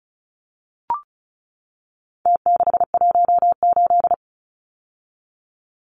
Morse code, heard as a keyed tone, about 34 wpm, 700 Hz. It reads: T618
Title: Numbers - Contest Style - 4-Digit 35wpm